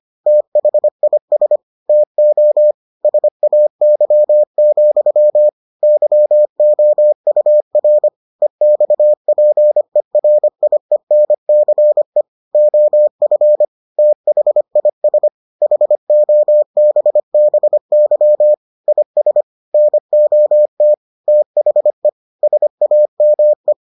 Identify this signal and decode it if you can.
Morse code, heard as a keyed tone, about 25 wpm, 600 Hz. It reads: THIS TO SAY, YOUR EXPERIENCE OF THIS HOBBY IS NOT THE SAME